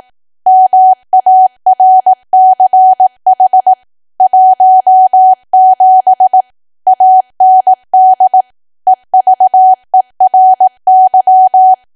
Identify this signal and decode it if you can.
Morse code, heard as a keyed tone, about 18 wpm, 750 Hz. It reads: MARCH 17 AND EVERY
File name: cwa / W1AW-Level2 / 18WPM